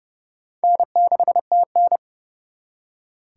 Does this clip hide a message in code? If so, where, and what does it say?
Yes: Morse code from 0.6 to 2.0 s, heard as a keyed tone, about 39 wpm, 700 Hz. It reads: N6TN